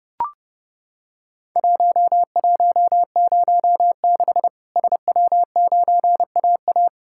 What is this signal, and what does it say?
Morse code, heard as a keyed tone, about 28 wpm, 700 Hz. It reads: 1106 SW9AA
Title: Worked All Europe DX Contest - QTC Exchange - 30wpm